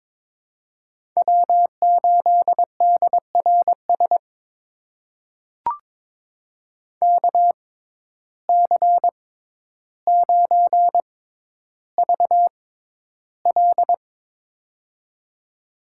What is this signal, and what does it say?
Morse code, heard as a keyed tone, about 22 wpm, spaced as for 11 wpm, 700 Hz. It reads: W8DRS KC9VL
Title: USA Call Signs - 5 Characters - 6x Character Spacing - 22wpm